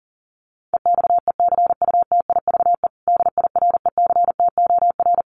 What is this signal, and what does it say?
Morse code, heard as a keyed tone, about 40 wpm, 700 Hz. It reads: EXECUTIVE DIRECTOR
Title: Sets of 5 Words Encore2 40wpm